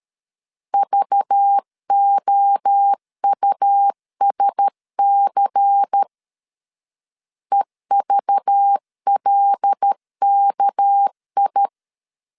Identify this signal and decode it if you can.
Morse code, heard as a keyed tone, about 13 wpm, 800 Hz. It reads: VOUSC EVLKI
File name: LEZIONI / 7 GF9Z / 7 GF9Z 12wpm 2147